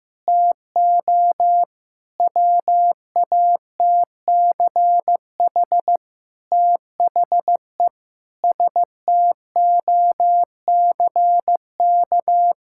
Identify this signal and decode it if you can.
Morse code, heard as a keyed tone, about 15 wpm, 700 Hz. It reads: TO WATCH THE STOCK
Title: Sentences from Top 1000 Words - Review Part 6 - 15wpm